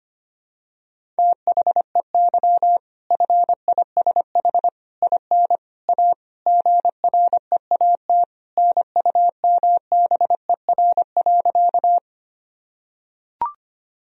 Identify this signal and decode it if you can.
Morse code, heard as a keyed tone, about 25 wpm, 700 Hz. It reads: THEY FISH IN A GREAT NUMBER.
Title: Sentences from Top 1000 Words - Review Part 1 - 25wpm